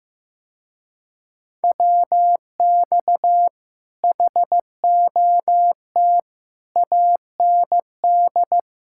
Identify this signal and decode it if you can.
Morse code, heard as a keyed tone, about 15 wpm, 700 Hz. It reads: WX HOT AND
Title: Speed-Racing Common QSO Phrases Encore - 15wpm